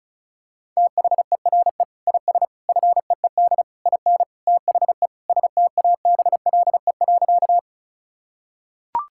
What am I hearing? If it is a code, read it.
Morse code, heard as a keyed tone, about 36 wpm, 700 Hz. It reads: THERE IS FEED IN THE STABLE.
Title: Sentences from Top 1600 Words 35wpm